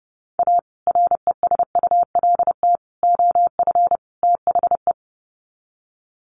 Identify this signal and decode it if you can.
Morse code, heard as a keyed tone, about 30 wpm, 700 Hz. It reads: A RESULT OF THE
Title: Sets of 4 Words 30wpm